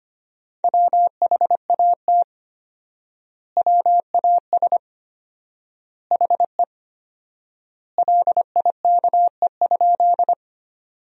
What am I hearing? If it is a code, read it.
Morse code, heard as a keyed tone, about 25 wpm, 700 Hz. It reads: WHAT WAS HE LIKE?
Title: 4x Word-Spacing - Sentences from Top 100 Words - 25wpm